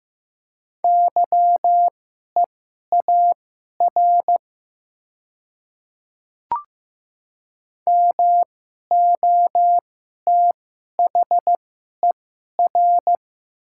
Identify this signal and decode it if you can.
Morse code, heard as a keyed tone, about 15 wpm, spaced as for 11 wpm, 700 Hz. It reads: YEAR MOTHER
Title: 2x Character Spacing - Top 200 Words - 15wpm